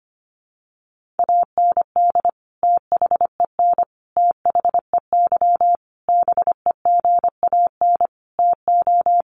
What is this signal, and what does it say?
Morse code, heard as a keyed tone, about 25 wpm, 700 Hz. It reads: AND THEN THEY BEGAN TO
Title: Sentences from Top 300 Words 25wpm